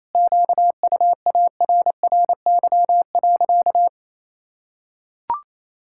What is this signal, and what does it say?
Morse code, heard as a keyed tone, about 27 wpm, 700 Hz. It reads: QUARRY.
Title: Pangram Challenge v2 - 28wpm